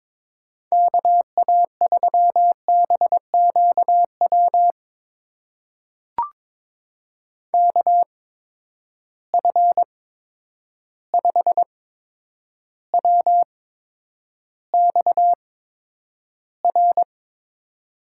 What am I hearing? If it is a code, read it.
Morse code, heard as a keyed tone, about 22 wpm, spaced as for 9 wpm, 700 Hz. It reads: KA3BQW KF5WXR
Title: USA Call Signs - 6 Characters - 8x Character Spacing - 22wpm